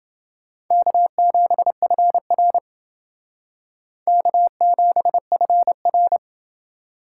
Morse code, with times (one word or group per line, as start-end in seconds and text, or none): 0.7-2.6 K7FR
4.1-6.2 K7FR